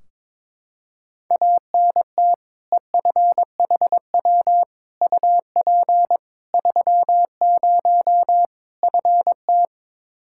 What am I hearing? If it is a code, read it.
Morse code, heard as a keyed tone, about 23 wpm, 700 Hz. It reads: ANT EFHW UP 30 FT
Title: Practice QSO - Exchange 2 - Info - 22wpm